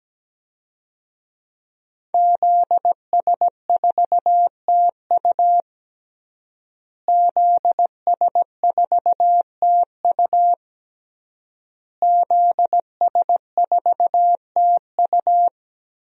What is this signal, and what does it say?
Morse code, heard as a keyed tone, about 17 wpm, 700 Hz. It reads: ZS4TU ZS4TU ZS4TU
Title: DXCC-100 Call Signs - 5 Characters - Repeated 3x - 17wpm